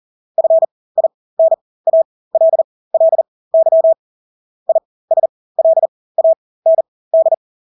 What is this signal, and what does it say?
Morse code, heard as a keyed tone, about 40 wpm, spaced as for 20 wpm, 650 Hz. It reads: FINALLY ISLAND
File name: Engelske ord 40-20 wpm